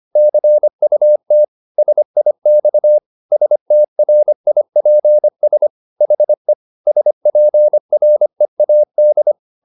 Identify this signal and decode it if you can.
Morse code, heard as a keyed tone, about 25 wpm, 600 Hz. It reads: CUT SIX STRIPS HE SPREAD
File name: the old man and the sea part 1 25wpm